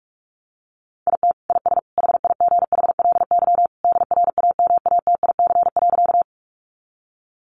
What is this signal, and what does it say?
Morse code, heard as a keyed tone, about 44 wpm, 700 Hz. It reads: IT IS HIGHLY DRAMATIC.
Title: Sentences from Top 1300 Words 45wpm